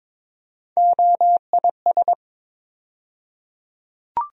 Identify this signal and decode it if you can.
Morse code, heard as a keyed tone, about 23 wpm, 700 Hz. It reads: OIS
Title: USA Call Sign Postfixes - 3 Letters 22wpm